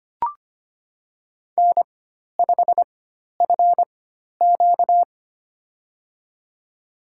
Morse code, heard as a keyed tone, about 25 wpm, spaced as for 12 wpm, 700 Hz. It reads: N5FQ
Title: USA Call Signs - 4 Characters - 4x Character Spacing - 25wpm